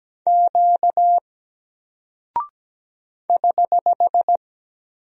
Morse code, heard as a keyed tone, about 17 wpm, 700 Hz. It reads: Q <HH>
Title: Single Character (Letter:Number:Punctuation:Prosign) Alt 17wpm